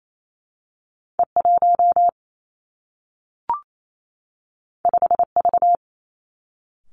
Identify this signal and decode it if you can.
Morse code, heard as a keyed tone, about 28 wpm, 700 Hz. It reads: E 1 5 V
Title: Two Challenging Letter-Number Combinations 28wpm